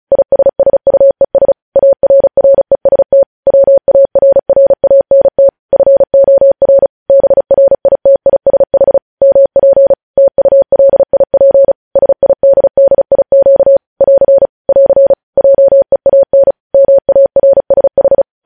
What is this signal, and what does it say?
Morse code, heard as a keyed tone, about 36 wpm, 550 Hz. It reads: ISSUES ARREST WARRANT FOR BRITISH MP TULIP SIDDIQ <AR> <AR> JEAN MARSH
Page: news headlines in morse code at 35 wpm